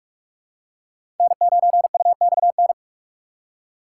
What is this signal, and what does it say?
Morse code, heard as a keyed tone, about 46 wpm, 700 Hz. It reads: N9UXN
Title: USA Call Signs - 5 Characters - 4x Character Spacing - 45wpm